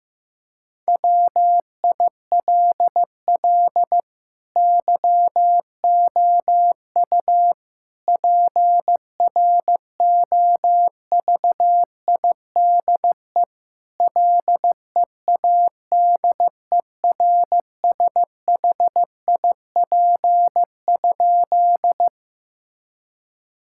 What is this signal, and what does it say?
Morse code, heard as a keyed tone, about 15 wpm, 700 Hz. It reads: WILL YOU PROVIDE LEADERSHIP?
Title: Sentences from Top 1600 Words 15wpm